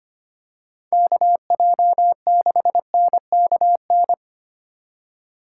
Morse code, heard as a keyed tone, about 25 wpm, 700 Hz. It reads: KJ6NKN
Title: USA Call Signs - 6 Characters - 4x Character Spacing - 25wpm